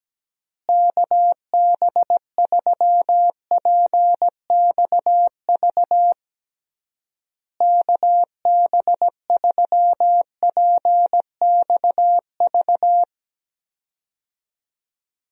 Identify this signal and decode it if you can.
Morse code, heard as a keyed tone, about 17 wpm, 700 Hz. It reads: KB3PXV KB3PXV
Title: USA Call Signs - 6 Characters - Repeated 3x - 17wpm